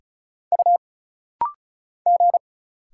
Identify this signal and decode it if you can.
Morse code, heard as a keyed tone, about 36 wpm, 700 Hz. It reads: U G
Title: Single Letters - Rapid-Fire 35wpm